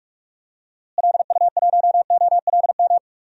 Morse code, heard as a keyed tone, about 41 wpm, 700 Hz. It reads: LU1OLM
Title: DXCC-100 Call Signs - 6 Characters - Repeated 3x - 45wpm